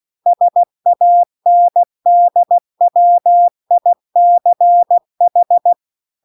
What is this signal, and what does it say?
Morse code, heard as a keyed tone, about 16 wpm, 700 Hz. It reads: SANDWICH